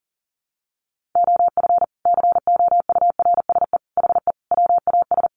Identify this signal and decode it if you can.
Morse code, heard as a keyed tone, about 40 wpm, 700 Hz. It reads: OF COURSE HE WAS
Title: Sentences from Top 500 Words 40wpm